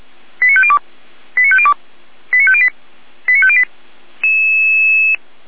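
Five-tone selective call (ZVEI), heard as quick runs of five tones, tones of about 70 ms. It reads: 89502 89502 89508 89508
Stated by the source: Received on an SDR, ZVEI five-tone selective call, channel 455.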